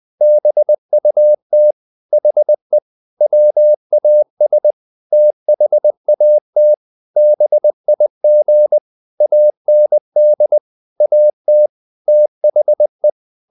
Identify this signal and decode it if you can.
Morse code, heard as a keyed tone, about 20 wpm, 600 Hz. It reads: BUT HE WAS THAT BIG AND AT THE
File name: the old man and the sea part 2 20wpm